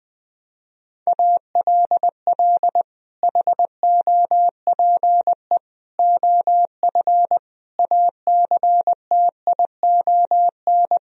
Morse code, heard as a keyed tone, about 20 wpm, 700 Hz. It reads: ALL HOPE OF ACTION
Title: Sentences from Top 1000 Words - Review Part 4 - 20wpm